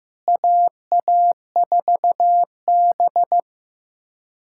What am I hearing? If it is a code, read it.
Morse code, heard as a keyed tone, about 15 wpm, 700 Hz. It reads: AA4B
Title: USA Callsign DE Callsign - 4 Characters - 15wpm